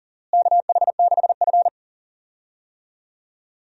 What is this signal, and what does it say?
Morse code, heard as a keyed tone, about 40 wpm, 700 Hz. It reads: KH6F
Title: DXCC-100 Call Signs - 4 Characters - Repeated 2x - 40wpm